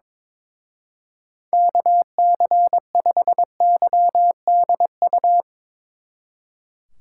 Morse code, heard as a keyed tone, about 24 wpm, 700 Hz. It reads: KC5YDU
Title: Speed Racing - USA 2x3 Call Signs 22wpm